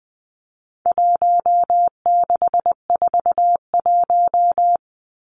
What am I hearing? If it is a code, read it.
Morse code, heard as a keyed tone, about 20 wpm, 700 Hz. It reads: 1641
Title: Numbers - Contest Style - 4-Digit 20wpm